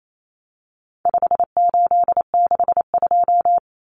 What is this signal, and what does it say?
Morse code, heard as a keyed tone, about 28 wpm, 700 Hz. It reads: 5862
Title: Common Cut Numbers - 4-Digit 28wpm